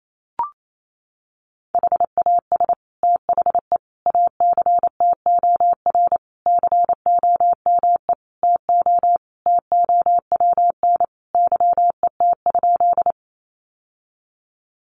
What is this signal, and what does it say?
Morse code, heard as a keyed tone, about 28 wpm, 700 Hz. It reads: HAS THE ACTOR COME TO TOWN YET?
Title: Sentences from Top 1500 Words 28wpm